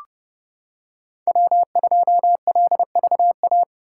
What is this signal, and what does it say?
Morse code, heard as a keyed tone, about 29 wpm, 700 Hz. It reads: W2LVA